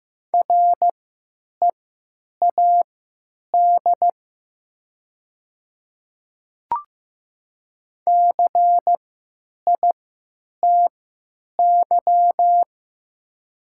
Morse code, heard as a keyed tone, about 15 wpm, spaced as for 9 wpm, 700 Hz. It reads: READ CITY
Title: 3x Character Spacing - Top 300 Words - 15wpm